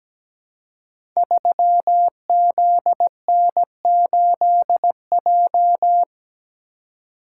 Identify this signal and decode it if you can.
Morse code, heard as a keyed tone, about 17 wpm, 700 Hz. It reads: 3ZN8J
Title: Code Groups - 5 Letter:Numbers - 17wpm